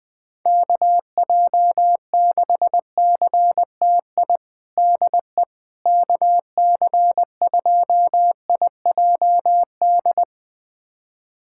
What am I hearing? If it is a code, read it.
Morse code, heard as a keyed tone, about 20 wpm, 700 Hz. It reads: KJ6CTI DE KC2IJD
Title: USA Callsign DE Callsign - 6 Characters - 20wpm